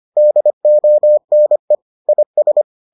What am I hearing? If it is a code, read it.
Morse code, heard as a keyed tone, about 25 wpm, 600 Hz. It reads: DONE IS